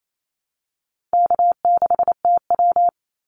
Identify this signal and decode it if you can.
Morse code, heard as a keyed tone, about 28 wpm, 700 Hz. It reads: K6TW
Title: NA SOTA Activator Callsigns 28wpm